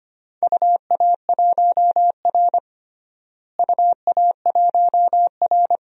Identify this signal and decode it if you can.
Morse code, heard as a keyed tone, about 25 wpm, 700 Hz. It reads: UA1R UA1R
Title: DXCC-100 Call Signs - 4 Characters - Repeated 3x - 25wpm